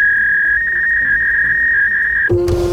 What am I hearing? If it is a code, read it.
Fax machine handshake.